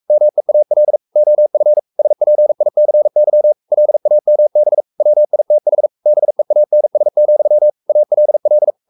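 Morse code, heard as a keyed tone, about 44 wpm, 600 Hz. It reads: MEAL OF SPICY LAMB WITH BEANS, ALL
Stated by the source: Morse code at 43 wpm sending Ken Follett's novel Never in English, from CW med Gnister.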